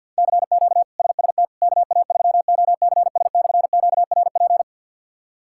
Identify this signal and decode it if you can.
Morse code, heard as a keyed tone, about 46 wpm, 700 Hz. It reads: CQ SST KA3OCS/QRP
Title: K1USN SST QSO Practice - 50wpm